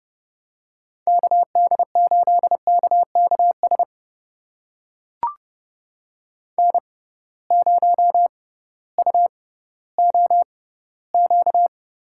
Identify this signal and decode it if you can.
Morse code, heard as a keyed tone, about 31 wpm, spaced as for 16 wpm, 700 Hz. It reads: KD8KKS N0UOQ
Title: USA Call Signs - 6x Character Spacing - 30wpm